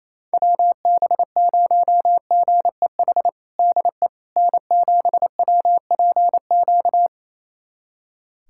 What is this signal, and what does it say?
Morse code, heard as a keyed tone, about 28 wpm, 700 Hz. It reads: WB0GEH DE N7WPQ
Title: USA Callsign DE Callsign 28wpm